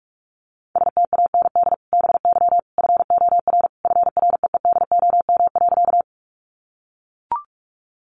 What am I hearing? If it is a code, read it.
Morse code, heard as a keyed tone, about 44 wpm, 700 Hz. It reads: STAND BY FOR FREEDOM.
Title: Sentences from Top 800 Words 45wpm